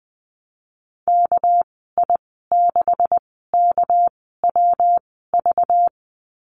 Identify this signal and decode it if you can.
Morse code, heard as a keyed tone, about 20 wpm, spaced as for 14 wpm, 700 Hz. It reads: KI6KWV
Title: 245 Lesson 39g - Letter z - Call Signs 20wpm